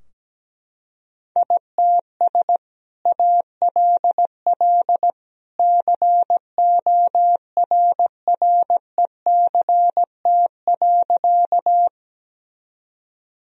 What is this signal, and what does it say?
Morse code, heard as a keyed tone, about 17 wpm, 700 Hz. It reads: ITS ALL CORRECT.
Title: Sentences from Top 1000 Words - Review Part 2 - 17wpm